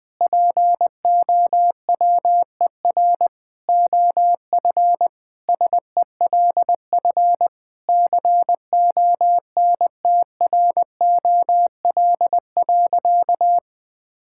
Morse code, heard as a keyed tone, about 20 wpm, 700 Hz. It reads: POWER OF SELF CONTROL.
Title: Sentences from Top 1000 Words - Review Part 5 - 20wpm